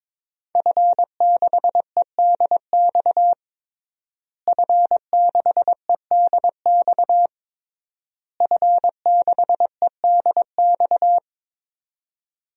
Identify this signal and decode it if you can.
Morse code, heard as a keyed tone, about 22 wpm, 700 Hz. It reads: F6EDX F6EDX F6EDX
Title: DXCC-100 Call Signs - 5 Characters - Repeated 3x - 22wpm